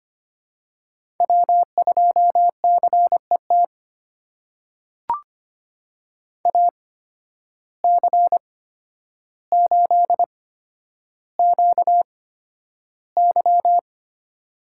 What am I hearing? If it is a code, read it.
Morse code, heard as a keyed tone, about 25 wpm, spaced as for 11 wpm, 700 Hz. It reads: W2CET AC8QY
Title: USA Call Signs - 8x Character Spacing - 25wpm